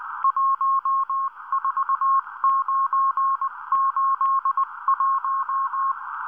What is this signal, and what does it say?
Morse code, heard as a keyed tone, about 19 wpm, 1080 Hz. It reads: 14981